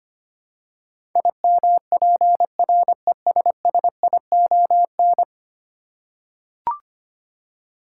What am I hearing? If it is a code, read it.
Morse code, heard as a keyed tone, about 25 wpm, 700 Hz. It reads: IMPRESSION
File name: 10 Letter Words - First 3 Letters Given - 25wpm